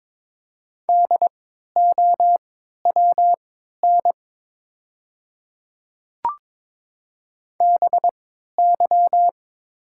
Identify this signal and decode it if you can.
Morse code, heard as a keyed tone, about 22 wpm, spaced as for 12 wpm, 700 Hz. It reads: DOWN BY
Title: 3x Character Spacing - Top 100 Words - 22wpm